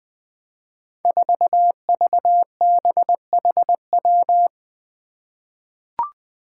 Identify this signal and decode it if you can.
Morse code, heard as a keyed tone, about 20 wpm, 700 Hz. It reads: 4VBHW
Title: Five Challenging Letter-Number Combinations 20wpm